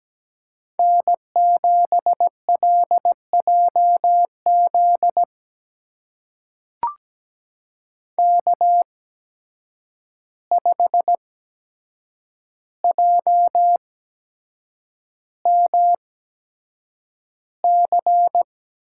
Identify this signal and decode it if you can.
Morse code, heard as a keyed tone, about 17 wpm, spaced as for 7 wpm, 700 Hz. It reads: N7LJZ K5JMC
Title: USA Call Signs - 5 Characters - 8x Character Spacing - 17wpm